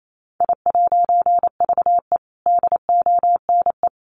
Morse code, heard as a keyed tone, about 28 wpm, 700 Hz. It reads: I'VE DONE
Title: A Sound Of Thunder 28wpm